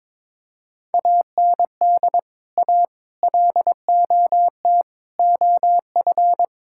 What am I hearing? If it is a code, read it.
Morse code, heard as a keyed tone, about 21 wpm, 700 Hz. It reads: AND A LOT OF
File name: practice / 22wpm / Build A Phrase - 5-Word Phrases - 22wpm